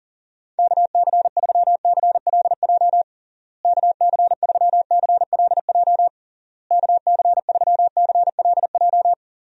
Morse code, heard as a keyed tone, about 40 wpm, 700 Hz. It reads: KC3CLJ KC3CLJ KC3CLJ
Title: USA Call Signs - 6 Characters - Repeated 3x - 40wpm